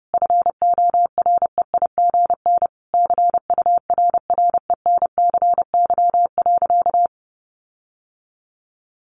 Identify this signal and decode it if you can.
Morse code, heard as a keyed tone, about 30 wpm, 700 Hz. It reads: FOREIGN CURRENCY.
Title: Sentences from Top 1500 Words 30wpm